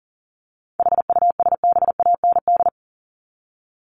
Morse code, heard as a keyed tone, about 38 wpm, 700 Hz. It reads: HUSBAND
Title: Top 1501-1750 Words 40wpm